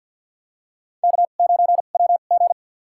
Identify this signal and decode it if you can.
Morse code, heard as a keyed tone, about 48 wpm, spaced as for 35 wpm, 700 Hz. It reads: K9WG
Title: USA Call Signs - 4 Characters - 2x Character Spacing - 50wpm